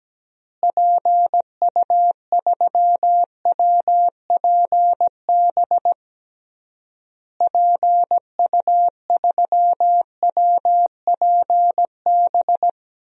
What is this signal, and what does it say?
Morse code, heard as a keyed tone, about 17 wpm, 700 Hz. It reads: PU3WPB PU3WPB
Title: DXCC-100 Call Signs - 6 Characters - Repeated 3x - 17wpm